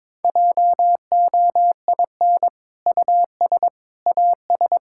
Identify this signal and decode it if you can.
Morse code, heard as a keyed tone, about 22 wpm, 700 Hz. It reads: JOIN US AS
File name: practice / 22wpm / Sentences with 2-Word Phrases - 22wpm